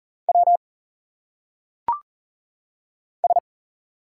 Morse code, heard as a keyed tone, about 40 wpm, 700 Hz. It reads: W S
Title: Single Letters - Phonetic Pronunciation 40wpm